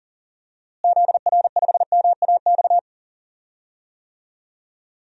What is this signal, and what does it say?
Morse code, heard as a keyed tone, about 40 wpm, 700 Hz. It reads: ZR5MAX
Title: DXCC-100 Call Signs - 6 Characters - Repeated 3x - 40wpm